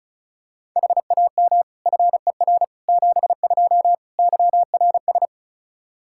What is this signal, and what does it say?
Morse code, heard as a keyed tone, about 36 wpm, 700 Hz. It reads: HAM FER 72 YRS
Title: K1USN SST QSO Practice 35wpm